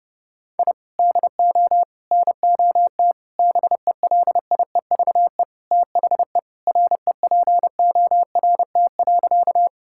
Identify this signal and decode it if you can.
Morse code, heard as a keyed tone, about 30 wpm, 700 Hz. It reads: I DO NOT BELIEVE THE REPORT.
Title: Sentences from Top 1000 Words - Review Part 6 - 30wpm